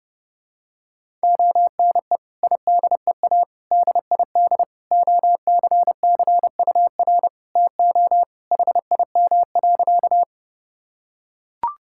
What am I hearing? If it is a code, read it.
Morse code, heard as a keyed tone, about 29 wpm, 700 Hz. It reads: ONE IDEA DID OCCUR TO HIM.
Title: Sentences from Top 1000 Words - Review Part 6 - 30wpm